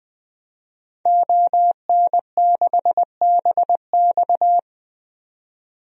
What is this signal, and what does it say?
Morse code, heard as a keyed tone, about 20 wpm, 700 Hz. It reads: ON6BX
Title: DXCC-100 Call Signs - 5 Characters - 20wpm